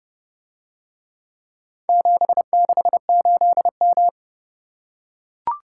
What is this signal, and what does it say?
Morse code, heard as a keyed tone, about 30 wpm, 700 Hz. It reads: 768M